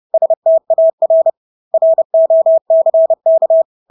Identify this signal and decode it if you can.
Morse code, heard as a keyed tone, about 30 wpm, 650 Hz. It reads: STAR ROCK
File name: Engelske ord 30-30 wpm